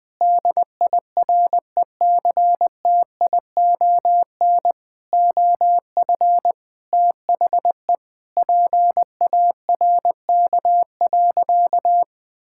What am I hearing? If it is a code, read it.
Morse code, heard as a keyed tone, about 20 wpm, 700 Hz. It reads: DIRECTION OF THE PARK.
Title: Sentences with 5-Word Phrases - 20wpm